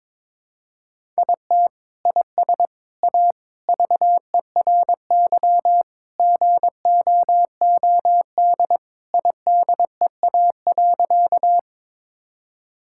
Morse code, heard as a keyed tone, about 22 wpm, 700 Hz. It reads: IT IS A VERY GOOD IDEA.
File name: practice / 22wpm / Sentences with 3-Word Phrases - 22wpm